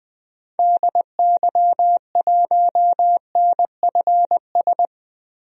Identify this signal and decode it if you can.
Morse code, heard as a keyed tone, about 21 wpm, 700 Hz. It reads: DY1NFS
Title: DXCC-100 Call Signs - 6 Characters - 20wpm